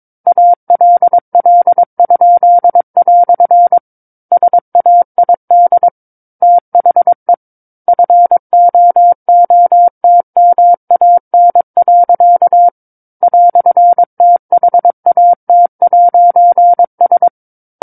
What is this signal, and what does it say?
Morse code, heard as a keyed tone, about 22 wpm, 700 Hz. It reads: ALL?" SAID THE FOOTMAN. "THAT'S
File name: 06-Chapter-06 22 WPM